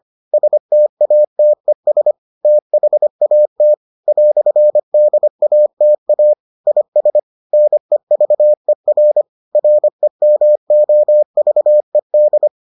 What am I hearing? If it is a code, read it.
Morse code, heard as a keyed tone, about 25 wpm, 600 Hz. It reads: STATES THAT "DATA IS NEVER REMOVED